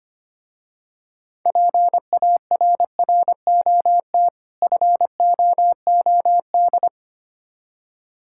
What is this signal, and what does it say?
Morse code, heard as a keyed tone, about 25 wpm, 700 Hz. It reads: PARROT FOOD
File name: practice / 25wpm / Difficult Words with Hints - 25wpm